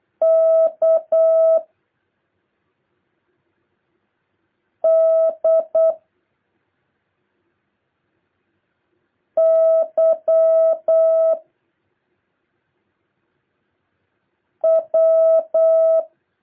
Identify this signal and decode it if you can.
Morse code, heard as a keyed tone, about 8 wpm, 650 Hz. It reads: K D Y W
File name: jh8cbh / 2023cw